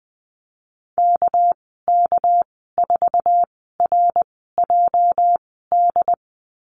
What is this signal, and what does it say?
Morse code, heard as a keyed tone, about 20 wpm, spaced as for 15 wpm, 700 Hz. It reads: KK4RJD
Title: 238 Lesson 38g - Letter q - Call Signs 20wpm